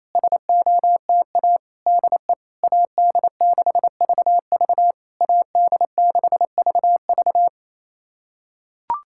Morse code, heard as a keyed tone, about 28 wpm, 700 Hz. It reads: SOTA DE AD6VV AD6VV
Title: SOTA QSO Practice 28wpm